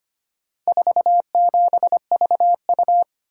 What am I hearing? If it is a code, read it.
Morse code, heard as a keyed tone, about 25 wpm, 700 Hz. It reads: 47VU
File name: practice / 25wpm / Four Letter-Number Combinations - 25wpm